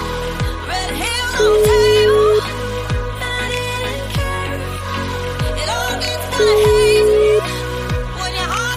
Ringback tone (Europe, 425 Hz), with noise.